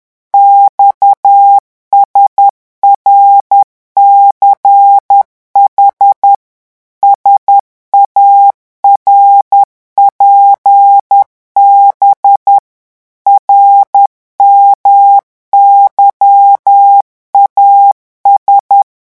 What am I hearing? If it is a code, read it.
Morse code, heard as a keyed tone, about 10 wpm, 800 Hz. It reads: XSRCH SARPB RMYAS
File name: morse mp3 / revision5-vitesse 10 mots